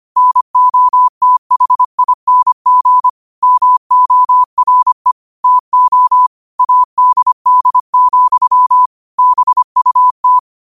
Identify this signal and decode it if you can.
Morse code, heard as a keyed tone, about 25 wpm, 1000 Hz. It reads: NOTHING MORE TO ADD, BUT